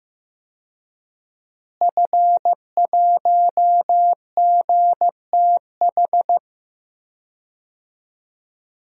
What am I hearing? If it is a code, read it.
Morse code, heard as a keyed tone, about 14 wpm, 700 Hz. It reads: F1GTH
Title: DXCC-100 Call Signs - 5 Characters - Repeated 3x - 15wpm